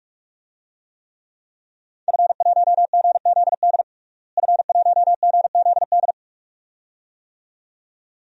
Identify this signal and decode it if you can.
Morse code, heard as a keyed tone, about 43 wpm, 700 Hz. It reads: F1GZD F1GZD
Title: DXCC-100 Call Signs - 5 Characters - Repeated 2x - 45wpm